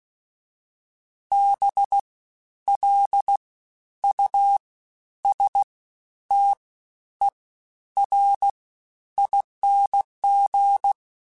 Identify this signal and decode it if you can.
Morse code, heard as a keyed tone, about 16 wpm, spaced as for 10 wpm, 800 Hz. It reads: BLUSTERING
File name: Special INGs-mp3s / Special INGs-mp3s / Fast ing @15